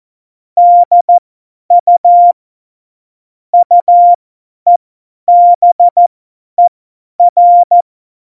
Morse code, heard as a keyed tone, about 14 wpm, spaced as for 10 wpm, 700 Hz. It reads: DU UEBER
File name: Morsekurs / Witze / Witz 07. Mai 2019, 50 F70